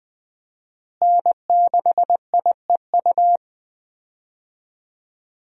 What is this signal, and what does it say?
Morse code, heard as a keyed tone, about 20 wpm, 700 Hz. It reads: N6IEU
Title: USA Call Signs - 5 Characters - Repeated 2x - 20wpm